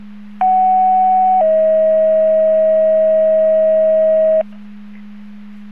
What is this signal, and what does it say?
Two-tone page, heard as a one-second tone, then a three-second tone, about 750 then 640 Hz. It is heard from 0.4 to 4.4 s.